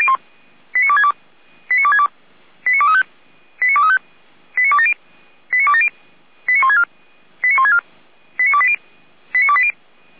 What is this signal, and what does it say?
Five-tone selective call (ZVEI), heard as quick runs of five tones, tones of about 70 ms. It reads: 89262 89262 89235 89235 89270 89270 89164 89164 89280 89280